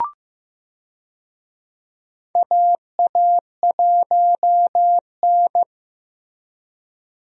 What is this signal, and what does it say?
Morse code, heard as a keyed tone, about 15 wpm, 700 Hz. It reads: AA1N
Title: USA Call Signs - 4 Characters - Repeated 3x - 15wpm